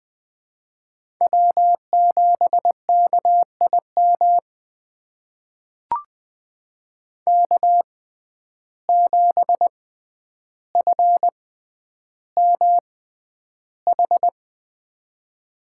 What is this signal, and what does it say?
Morse code, heard as a keyed tone, about 21 wpm, spaced as for 10 wpm, 700 Hz. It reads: W7KIM K7FMH
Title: USA Call Signs - 5 Characters - 6x Character Spacing - 20wpm